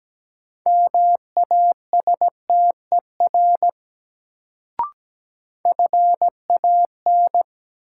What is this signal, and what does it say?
Morse code, heard as a keyed tone, about 17 wpm, 700 Hz. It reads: MASTER FAN
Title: Sentences from Top 1300 Words - New Word Drill Alt - 17wpm